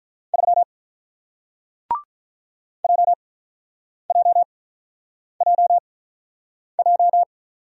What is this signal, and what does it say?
Morse code, heard as a keyed tone, about 47 wpm, spaced as for 9 wpm, 700 Hz. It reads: 3 JJJJ